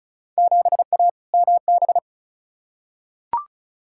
Morse code, heard as a keyed tone, about 36 wpm, 700 Hz. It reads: 7A MB
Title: Field Day Exchanges Encore - 35wpm